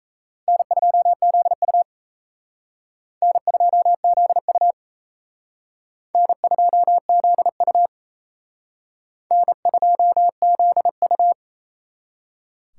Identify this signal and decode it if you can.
Morse code, heard as a keyed tone, about 36 wpm, 700 Hz. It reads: N2ZU N2ZU N2ZU N2ZU